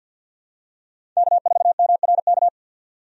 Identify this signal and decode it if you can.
Morse code, heard as a keyed tone, about 51 wpm, 700 Hz. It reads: K4MRK
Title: USA Call Signs - 5 Characters - 50wpm